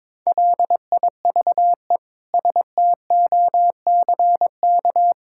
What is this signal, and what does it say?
Morse code, heard as a keyed tone, about 22 wpm, 700 Hz. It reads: LIVE STOCK
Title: Sentences from Top 1000 Words - Review Part 5 - 22wpm